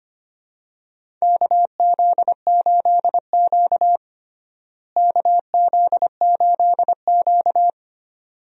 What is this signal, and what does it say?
Morse code, heard as a keyed tone, about 25 wpm, 700 Hz. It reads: KZ8Q KZ8Q
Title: USA Call Signs - 4 Characters - Repeated 2x - 25wpm